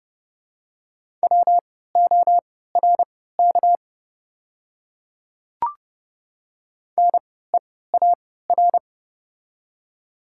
Morse code, heard as a keyed tone, about 30 wpm, spaced as for 17 wpm, 700 Hz. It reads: WORK NEAR
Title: 3x Character Spacing - Top 200 Words - 30wpm